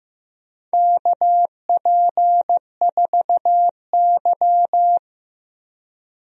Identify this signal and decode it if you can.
Morse code, heard as a keyed tone, about 15 wpm, 700 Hz. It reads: KP4Y
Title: DXCC-100 Call Signs - 4 Characters - Repeated 2x - 15wpm